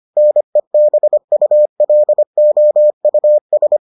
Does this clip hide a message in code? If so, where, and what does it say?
Yes: Morse code from 0.2 to 3.8 s, heard as a keyed tone, about 25 wpm, 600 Hz. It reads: NEBULOUS